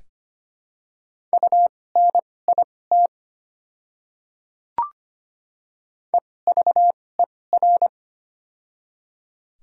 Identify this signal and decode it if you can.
Morse code, heard as a keyed tone, about 25 wpm, spaced as for 18 wpm, 700 Hz. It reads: UNIT EVER
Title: 2x Character Spacing - Top 500 Words - 25wpm